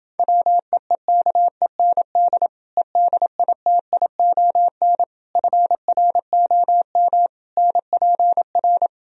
Morse code, heard as a keyed tone, about 29 wpm, 700 Hz. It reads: WEEKEND EDITION FROM NPR